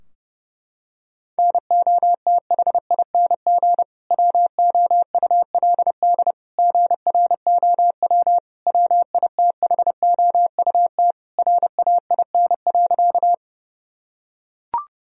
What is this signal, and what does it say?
Morse code, heard as a keyed tone, about 29 wpm, 700 Hz. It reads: NOTHING WOULD GROW WITHOUT RAIN.
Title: Sentences from Top 1000 Words - Review Part 3 - 30wpm